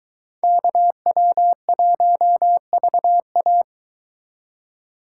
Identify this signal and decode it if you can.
Morse code, heard as a keyed tone, about 23 wpm, 700 Hz. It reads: KW1VA